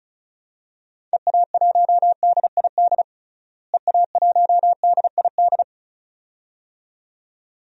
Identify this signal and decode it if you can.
Morse code, heard as a keyed tone, about 36 wpm, 700 Hz. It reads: EA1DID EA1DID
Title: DXCC-100 Call Signs - 6 Characters - Repeated 2x - 35wpm